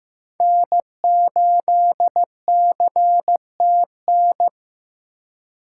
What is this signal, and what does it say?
Morse code, heard as a keyed tone, about 20 wpm, 700 Hz. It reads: N8CTN